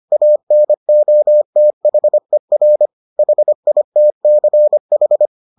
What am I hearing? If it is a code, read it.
Morse code, heard as a keyed tone, about 25 wpm, 600 Hz. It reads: ANOTHER HITCH